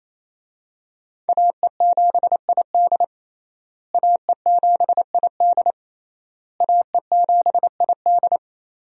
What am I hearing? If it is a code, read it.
Morse code, heard as a keyed tone, about 28 wpm, 700 Hz. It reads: AE7ID AE7ID AE7ID
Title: USA Call Signs - 5 Characters - Repeated 3x - 28wpm